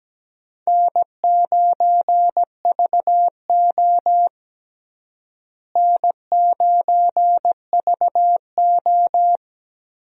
Morse code, heard as a keyed tone, about 17 wpm, 700 Hz. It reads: N9VO N9VO
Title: USA Call Signs - 4 Characters - Repeated 2x - 17wpm